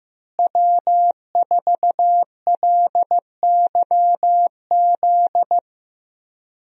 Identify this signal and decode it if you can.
Morse code, heard as a keyed tone, about 15 wpm, 700 Hz. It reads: W4LYZ
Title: USA Call Signs - 5 Characters - Repeated 3x - 15wpm